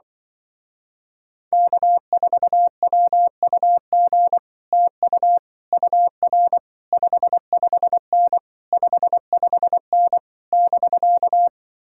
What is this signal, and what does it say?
Morse code, heard as a keyed tone, about 24 wpm, 700 Hz. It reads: K4WUG TU UR 55N 55N <BK>
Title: Speed-Racing Common QSO Phrases Encore - 20wpm